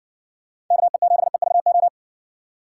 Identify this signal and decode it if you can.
Morse code, heard as a keyed tone, about 60 wpm, 700 Hz. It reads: KE7EVO